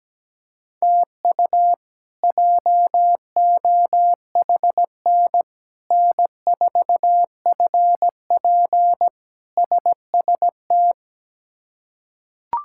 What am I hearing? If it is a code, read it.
Morse code, heard as a keyed tone, about 17 wpm, 700 Hz. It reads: TU JOHN N4FP SST
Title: K1USN SST QSO Practice - 17wpm